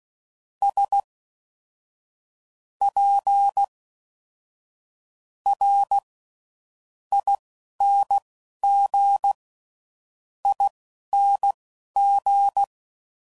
Morse code, heard as a keyed tone, about 16 wpm, spaced as for 7 wpm, 800 Hz. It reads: SPRINGING